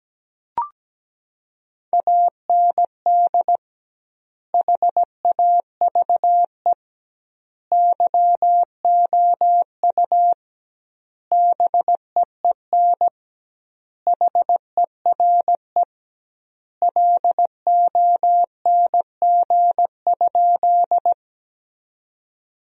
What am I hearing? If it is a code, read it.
Morse code, heard as a keyed tone, about 17 wpm, 700 Hz. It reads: AND HAVE YOU BEEN HERE LONG?
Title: 2x Word-Spacing - Sentences from Top 200 Words - 17wpm